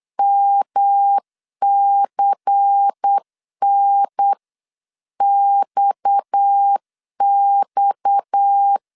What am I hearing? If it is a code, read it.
Morse code, heard as a keyed tone, about 8 wpm, 800 Hz. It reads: MCNXX
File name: LEZIONI / 6 PKX0 / 6 PKX 8wpm 3041